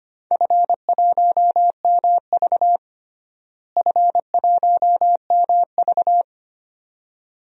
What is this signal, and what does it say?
Morse code, heard as a keyed tone, about 25 wpm, 700 Hz. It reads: F1MV F1MV
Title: DXCC-100 Call Signs - 4 Characters - Repeated 2x - 25wpm